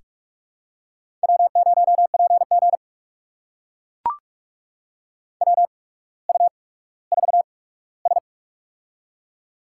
Morse code, heard as a keyed tone, about 44 wpm, spaced as for 19 wpm, 700 Hz. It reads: W0PG WU4S
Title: USA Call Signs - 8x Character Spacing - 45wpm